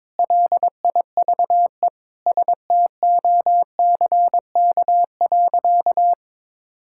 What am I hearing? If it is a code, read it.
Morse code, heard as a keyed tone, about 22 wpm, 700 Hz. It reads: LIVESTOCK.
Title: Sentences from Top 1000 Words - Review Part 5 - 22wpm